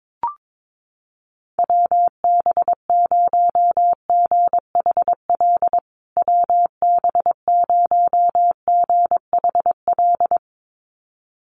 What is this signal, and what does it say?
Morse code, heard as a keyed tone, about 22 wpm, 700 Hz. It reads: WB0GHL WB0GHL
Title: USA Call Signs Repeated 2x 22wpm